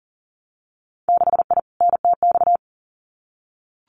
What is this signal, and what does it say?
Morse code, heard as a keyed tone, about 38 wpm, 700 Hz. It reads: 6I NTX
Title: Winter Field Day Exchanges Encore 40wpm